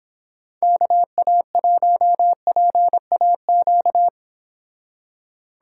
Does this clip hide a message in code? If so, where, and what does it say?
Yes: Morse code from 0.6 to 4.1 s, heard as a keyed tone, about 29 wpm, 700 Hz. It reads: KA1PAQ